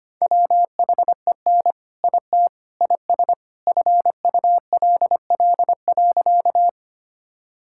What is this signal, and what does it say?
Morse code, heard as a keyed tone, about 25 wpm, 700 Hz. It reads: WHEN IT IS FULL.
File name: practice / 25wpm / All-Age Appropriate Jokes - 25wpm